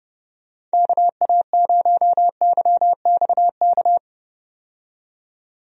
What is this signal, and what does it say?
Morse code, heard as a keyed tone, about 30 wpm, 700 Hz. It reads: KA0YXK
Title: USA Call Signs - 6 Characters - 30wpm